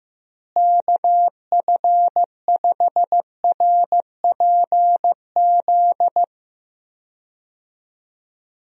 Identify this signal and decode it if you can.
Morse code, heard as a keyed tone, about 15 wpm, 700 Hz. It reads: KF5RPZ
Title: USA Call Signs - 6 Characters - Repeated 2x - 15wpm